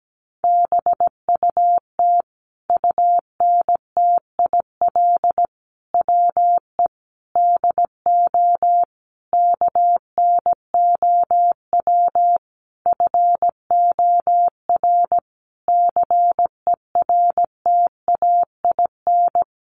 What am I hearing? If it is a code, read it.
Morse code, heard as a keyed tone, about 17 wpm, 700 Hz. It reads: BUT UNTIL WE DO KNOW FOR CERTAIN
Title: A Sound Of Thunder 17wpm